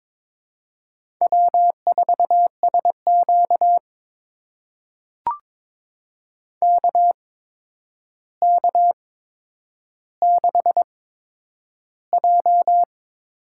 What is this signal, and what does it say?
Morse code, heard as a keyed tone, about 21 wpm, spaced as for 9 wpm, 700 Hz. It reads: W4SQ KK6J
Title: USA Call Signs - 4 Characters - 8x Character Spacing - 22wpm